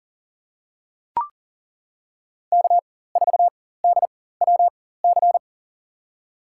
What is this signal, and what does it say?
Morse code, heard as a keyed tone, about 40 wpm, spaced as for 19 wpm, 700 Hz. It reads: K4DWC
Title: USA Call Signs - 5 Characters - 4x Character Spacing - 40wpm